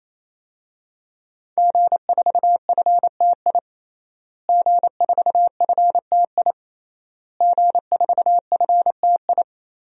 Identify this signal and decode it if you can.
Morse code, heard as a keyed tone, about 28 wpm, 700 Hz. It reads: G4FTI G4FTI G4FTI
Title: DXCC-100 Call Signs - 5 Characters - Repeated 3x - 28wpm